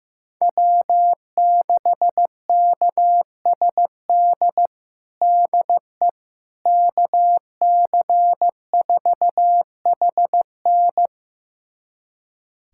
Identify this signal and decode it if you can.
Morse code, heard as a keyed tone, about 15 wpm, 700 Hz. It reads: W6KSD DE KC4HN
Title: USA Callsign DE Callsign - 5 Characters - 15wpm